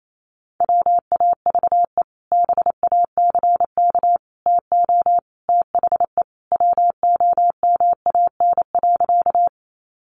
Morse code, heard as a keyed tone, about 28 wpm, 700 Hz. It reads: WAVE BACK TO THE WOMAN.
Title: Sentences from Top 1400 Words 28wpm